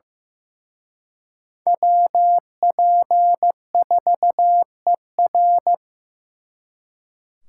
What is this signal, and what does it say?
Morse code, heard as a keyed tone, about 18 wpm, 700 Hz. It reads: WP4ER
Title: Speed Racing - USA 2x2 Call Signs 15wpm